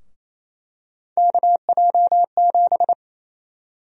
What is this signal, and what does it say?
Morse code, heard as a keyed tone, about 28 wpm, 700 Hz. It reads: KJ7
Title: USA Call Sign Prefixes - Repeated 2x - 28wpm